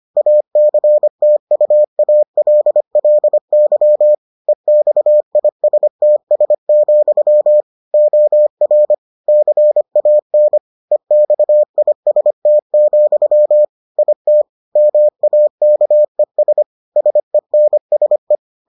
Morse code, heard as a keyed tone, about 25 wpm, 600 Hz. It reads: ACTUALLY EXISTS, OR CAN EXIST, IT MAKES SENSE